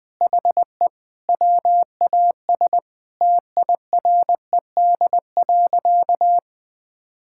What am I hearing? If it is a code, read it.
Morse code, heard as a keyed tone, about 21 wpm, 700 Hz. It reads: HE WAS TIRED.
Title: Sentences with 3-Word Phrases - 20wpm